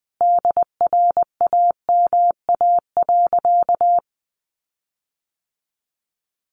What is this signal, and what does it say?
Morse code, heard as a keyed tone, about 20 wpm, 700 Hz. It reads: DRAMA.
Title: Sentences from Top 1500 Words 20wpm